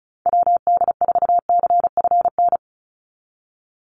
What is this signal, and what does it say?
Morse code, heard as a keyed tone, about 34 wpm, 700 Hz. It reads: WD4CFN
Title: NA SOTA Chaser Callsigns 35wpm